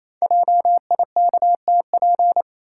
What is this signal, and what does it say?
Morse code, heard as a keyed tone, about 27 wpm, 700 Hz. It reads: JIKTP
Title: Code Groups - 5 Letters - 28wpm